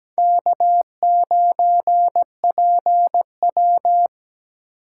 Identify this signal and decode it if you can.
Morse code, heard as a keyed tone, about 20 wpm, 700 Hz. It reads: K9PW
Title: Speed Racing - USA 1x2 Call Signs 17wpm